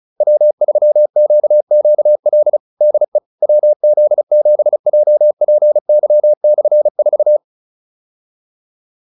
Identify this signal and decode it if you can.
Morse code, heard as a keyed tone, about 34 wpm, 600 Hz. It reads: W3QQL DE WZ7JPY/4